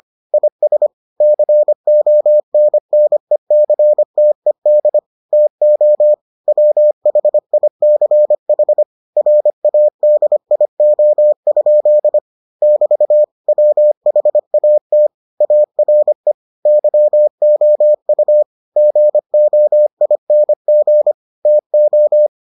Morse code, heard as a keyed tone, about 25 wpm, 600 Hz. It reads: IS CONNECTED TO WHICH RADIO? <BT> WHAT ARE YOU GOING TO